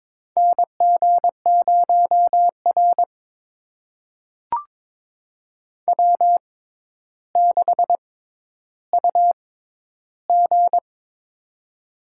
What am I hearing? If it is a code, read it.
Morse code, heard as a keyed tone, about 22 wpm, spaced as for 11 wpm, 700 Hz. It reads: NG0R W6UG
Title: USA Call Signs - 6x Character Spacing - 22wpm